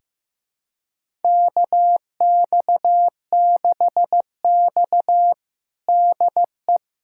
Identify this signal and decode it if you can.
Morse code, heard as a keyed tone, about 16 wpm, 700 Hz. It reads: KX6X DE
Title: USA Callsign DE Callsign - 4 Characters - 15wpm